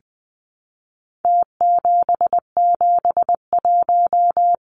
Morse code, heard as a keyed tone, about 20 wpm, 700 Hz. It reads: T771
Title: Numbers - Contest Style - 4-Digit 20wpm